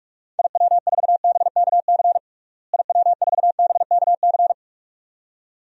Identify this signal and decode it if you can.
Morse code, heard as a keyed tone, about 46 wpm, 700 Hz. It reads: IW4BKC IW4BKC
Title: DXCC-100 Call Signs - 6 Characters - Repeated 2x - 45wpm